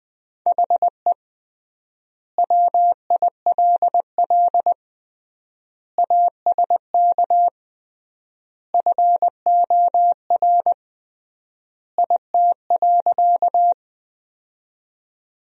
Morse code, heard as a keyed tone, about 20 wpm, 700 Hz. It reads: HE WILL ASK FOR IT.
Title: 3x Word-Spacing - Sentences from Top 200 Words - 20wpm